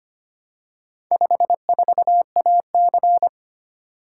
Morse code, heard as a keyed tone, about 25 wpm, 700 Hz. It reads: 54AC
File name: practice / 25wpm / Four Letter-Number Combinations - 25wpm